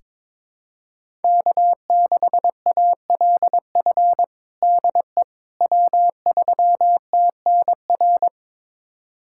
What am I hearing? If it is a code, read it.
Morse code, heard as a keyed tone, about 22 wpm, 700 Hz. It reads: K6ALF DE W3TNR
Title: USA Callsign DE Callsign - 5 Characters - 22wpm